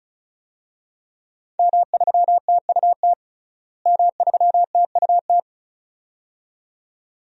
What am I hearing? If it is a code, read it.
Morse code, heard as a keyed tone, about 36 wpm, 700 Hz. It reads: M3TUT M3TUT
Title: DXCC-100 Call Signs - 5 Characters - Repeated 2x - 35wpm